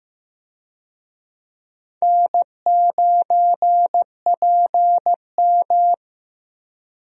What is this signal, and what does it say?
Morse code, heard as a keyed tone, about 16 wpm, 700 Hz. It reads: N9PM
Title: USA Call Signs - 4 Characters - Repeated 2x - 15wpm